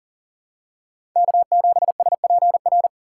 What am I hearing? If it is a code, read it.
Morse code, heard as a keyed tone, about 40 wpm, 700 Hz. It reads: K7SPR
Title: USA Call Signs - 5 Characters - 4x Character Spacing - 40wpm